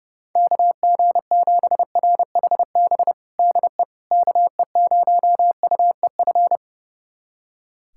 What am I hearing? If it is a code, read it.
Morse code, heard as a keyed tone, about 30 wpm, 700 Hz. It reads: KG7RHB DE KE0UEF
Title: USA Callsign DE Callsign - 6 Characters - 30wpm